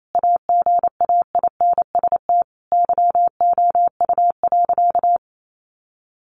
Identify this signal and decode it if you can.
Morse code, heard as a keyed tone, about 28 wpm, 700 Hz. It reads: AGAINST YOU.
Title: Sentences from Top 1200 Words 28wpm